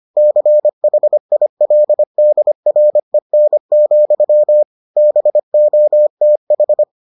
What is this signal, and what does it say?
Morse code, heard as a keyed tone, about 25 wpm, 600 Hz. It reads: CHILDREN, BOTH